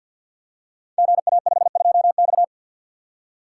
Morse code, heard as a keyed tone, about 51 wpm, 700 Hz. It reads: GA52X
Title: Code Groups - 5 Letter:Numbers - 50wpm